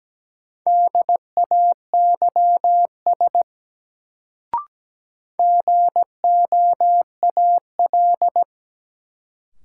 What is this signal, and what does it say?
Morse code, heard as a keyed tone, about 17 wpm, 700 Hz. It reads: DAYS GOAL
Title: Sentences from Top 1000 Words - Review Part 5 - New Word Drill - 17wpm